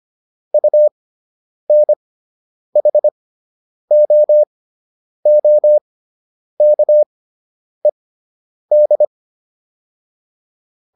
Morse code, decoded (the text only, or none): UNHOOKED